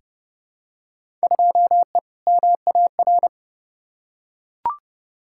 Morse code, heard as a keyed tone, about 30 wpm, 700 Hz. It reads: 2E MAR
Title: Field Day Exchanges Encore - 30wpm